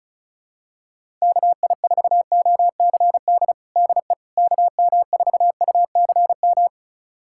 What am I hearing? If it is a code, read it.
Morse code, heard as a keyed tone, about 34 wpm, 700 Hz. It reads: KI4OCD DE KM4UCM
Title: USA Callsign DE Callsign - 6 Characters - 35wpm